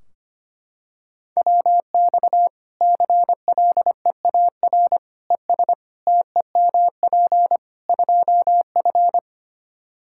Morse code, heard as a keyed tone, about 25 wpm, 700 Hz. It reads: WX CLEAR ES TEMP 2F
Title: Practice QSO - Exchange 2 - Info - 25wpm